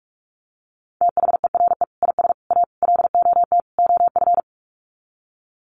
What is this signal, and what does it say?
Morse code, heard as a keyed tone, about 44 wpm, 700 Hz. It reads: THERE IS A LOT OF
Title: Sets of 5 Words 45wpm